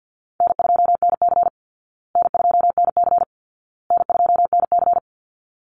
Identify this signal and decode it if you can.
Morse code, heard as a keyed tone, about 50 wpm, 700 Hz. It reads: N2NC N2NC N2NC
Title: Call Signs Repeated 2x - 50wpm